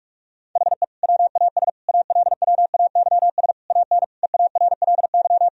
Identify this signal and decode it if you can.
Morse code, heard as a keyed tone, about 46 wpm, 700 Hz. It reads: HE WAS ALWAYS AN EARLY
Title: Sentences from Top 1000 Words - Review Part 2 - 45wpm